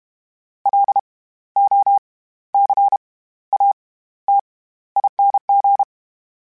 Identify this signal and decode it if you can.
Morse code, heard as a keyed tone, about 32 wpm, spaced as for 15 wpm, 800 Hz. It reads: LOCATING
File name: Special INGs / Special INGs-mp3s / SlowToFaster ing @30